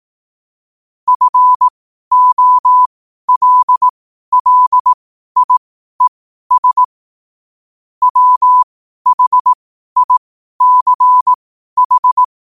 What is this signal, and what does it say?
Morse code, heard as a keyed tone, about 18 wpm, spaced as for 12 wpm, 1000 Hz. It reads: FOLLIES WHICH
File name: qotdfiles / 12wpm